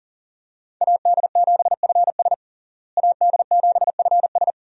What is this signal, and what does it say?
Morse code, heard as a keyed tone, about 40 wpm, 700 Hz. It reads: AD7FS AD7FS
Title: USA Call Signs - 5 Characters - Repeated 2x - 40wpm